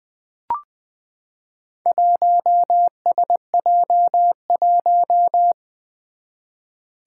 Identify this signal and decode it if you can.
Morse code, heard as a keyed tone, about 20 wpm, 700 Hz. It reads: 1SJ1
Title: Four Challenging Letter-Number Combinations 20wpm